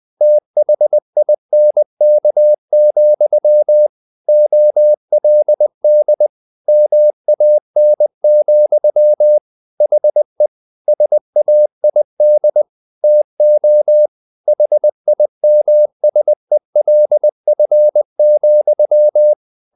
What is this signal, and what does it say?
Morse code, heard as a keyed tone, about 20 wpm, 600 Hz. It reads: THINK, OLD MAN, HE SAID TO HIMSELF,
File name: the old man and the sea / the old man and the sea part 1 20wpm